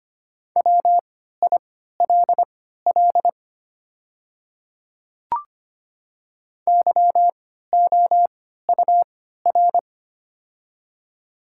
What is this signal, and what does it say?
Morse code, heard as a keyed tone, about 25 wpm, spaced as for 14 wpm, 700 Hz. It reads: WILL YOUR
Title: 3x Character Spacing - Top 200 Words - 25wpm